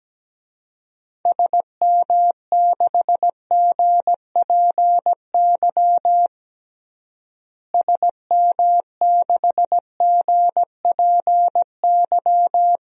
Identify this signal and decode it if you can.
Morse code, heard as a keyed tone, about 17 wpm, 700 Hz. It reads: SM6GPY SM6GPY
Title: DXCC-100 Call Signs - 6 Characters - Repeated 3x - 17wpm